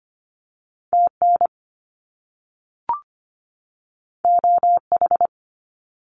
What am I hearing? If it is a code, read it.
Morse code, heard as a keyed tone, about 25 wpm, 700 Hz. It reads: TN OH
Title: US States 25wpm